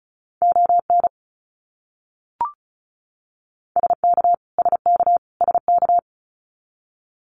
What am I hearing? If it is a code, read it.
Morse code, heard as a keyed tone, about 34 wpm, 700 Hz. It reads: ON SK SK SK
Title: Canadian Provinces and Territories - Repeated 3x - 35wpm